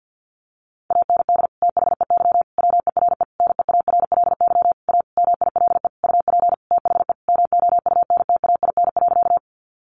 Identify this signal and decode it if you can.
Morse code, heard as a keyed tone, about 48 wpm, 700 Hz. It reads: AND THEY WERE NEARLY A MILE UP THE MOUNTAIN.
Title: Sentences from Top 1300 Words 50wpm